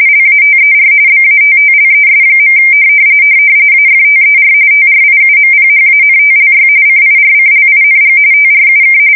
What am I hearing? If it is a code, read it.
RTTY, about 45.45 baud. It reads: LATER BATTLES. AFTER TRIUMPHANTLY CIRCLING TRUK
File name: fsk170 / USS-New-Jersey